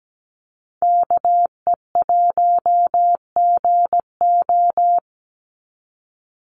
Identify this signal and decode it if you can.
Morse code, heard as a keyed tone, about 17 wpm, 700 Hz. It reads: KE1GO
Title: USA Call Signs 17wpm